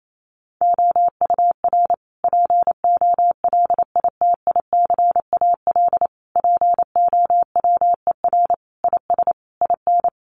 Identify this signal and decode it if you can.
Morse code, heard as a keyed tone, about 28 wpm, 700 Hz. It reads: OUR POLITICAL POWER IS IN
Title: Sentences from Top 900 Words 28wpm